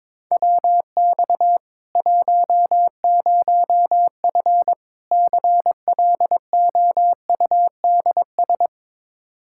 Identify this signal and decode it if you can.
Morse code, heard as a keyed tone, about 22 wpm, 700 Hz. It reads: WX 10F CLOUDS
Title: Practice QSO - Exchange 2 - Info - 22wpm